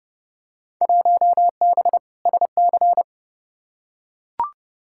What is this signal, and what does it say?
Morse code, heard as a keyed tone, about 30 wpm, 700 Hz. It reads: 1B SC
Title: Field Day Exchanges - 30wpm